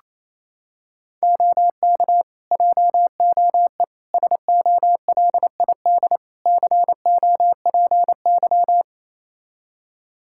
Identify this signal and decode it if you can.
Morse code, heard as a keyed tone, about 28 wpm, 700 Hz. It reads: OK JOE SOLID COPY
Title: Practice QSO - Exchange 3 - More Info - 28wpm